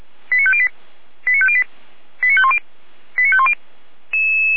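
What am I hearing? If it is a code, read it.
Five-tone selective call (ZVEI), heard as quick runs of five tones, tones of about 70 ms. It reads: 89508 89508 89510 89510
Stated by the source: ZVEI five-tone call received on SDR, recorded by feuerwehr-orlamuende (fire brigade Orlamünde).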